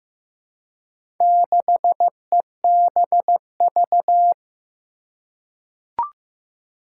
Morse code, heard as a keyed tone, about 15 wpm, 700 Hz. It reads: 6EBV
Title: Four Challenging Letter-Number Combinations 15wpm